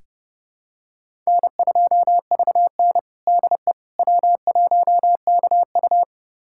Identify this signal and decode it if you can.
Morse code, heard as a keyed tone, about 30 wpm, 700 Hz. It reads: N2VN DE W1KU
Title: USA Callsign DE Callsign 30wpm